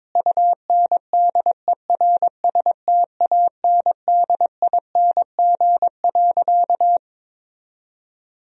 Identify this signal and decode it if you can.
Morse code, heard as a keyed tone, about 21 wpm, 700 Hz. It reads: UNDERSTANDING.
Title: Sentences from Top 1000 Words - Review Part 3 - 22wpm